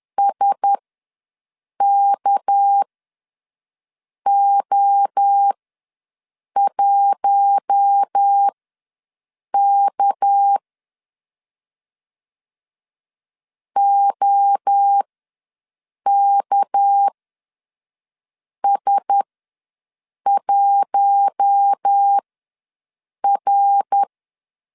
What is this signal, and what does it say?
Morse code, heard as a keyed tone, about 10 wpm, spaced as for 5 wpm, 800 Hz. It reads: SKO1K OKS1R
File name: LEZIONI / 6 PKX0 / 6 PK ws 10wpm 3745